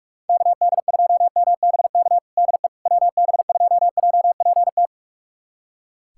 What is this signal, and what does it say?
Morse code, heard as a keyed tone, about 41 wpm, 700 Hz. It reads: KD2MBK DE WB2JPT
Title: USA Callsign DE Callsign 45wpm